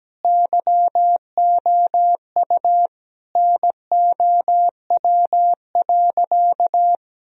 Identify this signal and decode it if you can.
Morse code, heard as a keyed tone, about 17 wpm, 700 Hz. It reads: YOU NOW.
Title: Sentences from Top 1000 Words - Review Part 2 - 17wpm